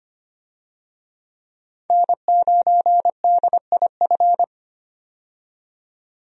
Morse code, heard as a keyed tone, about 25 wpm, 700 Hz. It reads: N9DIF
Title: USA Call Signs - 5 Characters - 25wpm